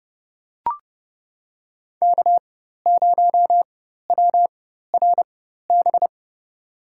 Morse code, heard as a keyed tone, about 30 wpm, spaced as for 14 wpm, 700 Hz. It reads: K0WRB
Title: USA Call Signs - 4x Character Spacing - 30wpm